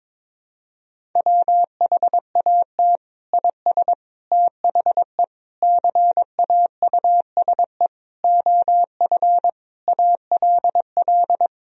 Morse code, heard as a keyed tone, about 22 wpm, 700 Hz. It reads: WHAT IS THE CAUSE OF ALL
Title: Sentences from Top 1000 Words - Review Part 1 - 22wpm